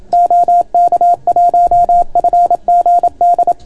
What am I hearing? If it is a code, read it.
Morse code, heard as a keyed tone, about 27 wpm, 690 Hz. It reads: OK1FGD